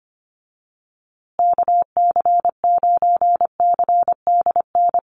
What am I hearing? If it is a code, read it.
Morse code, heard as a keyed tone, about 25 wpm, 700 Hz. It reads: KC9CDN
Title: USA Call Signs Repeated 2x 25wpm